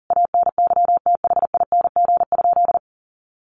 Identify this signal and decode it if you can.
Morse code, heard as a keyed tone, about 40 wpm, 700 Hz. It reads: ANYTHING?
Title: Sentences from Top 1200 Words 40wpm